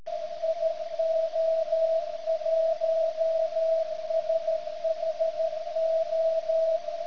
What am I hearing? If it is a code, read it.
Morse code, heard as a keyed tone, about 13 wpm, 650 Hz. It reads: IO1SHO
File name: ft8-vs-cw / 80-to-150 CW 100hz